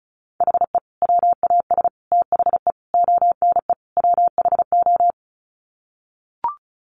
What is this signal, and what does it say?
Morse code, heard as a keyed tone, about 34 wpm, 700 Hz. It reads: HE WAS THE ONE WHO
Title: Sets of 5 Words Encore2 35wpm